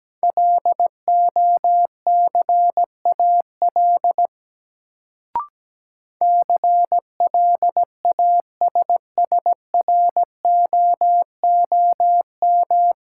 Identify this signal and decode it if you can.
Morse code, heard as a keyed tone, about 17 wpm, 700 Hz. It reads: LOCAL CLASSROOM
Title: Sentences from Top 1500 Words - New Word Drill Alt - 17wpm